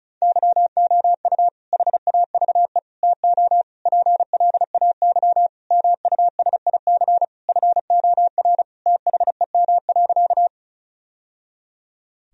Morse code, heard as a keyed tone, about 36 wpm, 700 Hz. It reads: YOU HAVE TO PLAY MUSIC FOR THEM.
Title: Sentences from Top 1000 Words - Review Part 1 - 35wpm